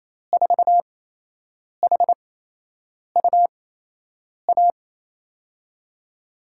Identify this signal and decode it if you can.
Morse code, heard as a keyed tone, about 28 wpm, spaced as for 8 wpm, 700 Hz. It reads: 4HUA
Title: USA Call Signs - 8x Character Spacing - 28wpm